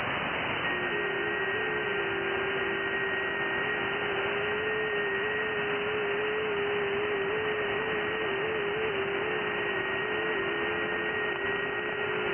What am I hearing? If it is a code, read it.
FT8.
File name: IC-7100